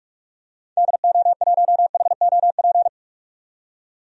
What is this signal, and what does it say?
Morse code, heard as a keyed tone, about 45 wpm, 700 Hz. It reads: DO1HOP
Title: DXCC-100 Call Signs - 6 Characters - 45wpm